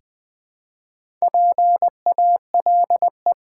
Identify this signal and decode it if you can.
Morse code, heard as a keyed tone, about 20 wpm, 700 Hz. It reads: PALE
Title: Build A Word - 1 Letter At A Time - 3 to 5-Letter Words - 20wpm